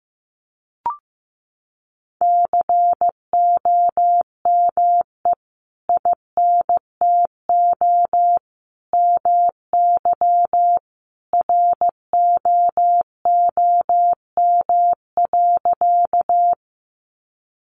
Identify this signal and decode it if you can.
Morse code, heard as a keyed tone, about 15 wpm, 700 Hz. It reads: COME INTO MY ROOM.
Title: Sentences from Top 600 Words 15wpm